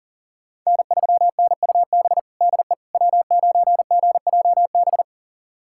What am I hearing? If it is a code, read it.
Morse code, heard as a keyed tone, about 40 wpm, 700 Hz. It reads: N3NUB DE W9GJB
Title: USA Callsign DE Callsign - 5 Characters - 40wpm